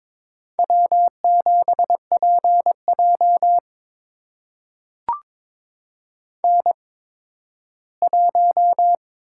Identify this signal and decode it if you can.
Morse code, heard as a keyed tone, about 22 wpm, spaced as for 13 wpm, 700 Hz. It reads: W7PJ N1
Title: USA Call Signs - 4 Characters - 8x Character Spacing - 22wpm